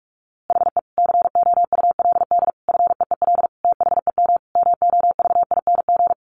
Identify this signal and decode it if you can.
Morse code, heard as a keyed tone, about 44 wpm, 700 Hz. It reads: HE COULD FEEL THEM MOVING
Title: A Sound Of Thunder 45wpm